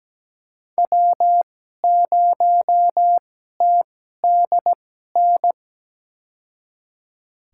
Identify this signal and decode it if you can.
Morse code, heard as a keyed tone, about 17 wpm, spaced as for 12 wpm, 700 Hz. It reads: W0TDN
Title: USA Call Signs - 5 Characters - 2x Character Spacing - 17wpm